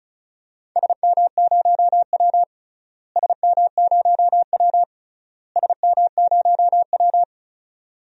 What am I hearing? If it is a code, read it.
Morse code, heard as a keyed tone, about 34 wpm, 700 Hz. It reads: SM0W SM0W SM0W
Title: DXCC-100 Call Signs - 4 Characters - Repeated 3x - 35wpm